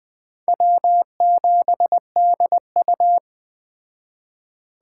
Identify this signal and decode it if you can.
Morse code, heard as a keyed tone, about 20 wpm, 700 Hz. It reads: W7DU
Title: Speed Racing - USA Call Signs 17wpm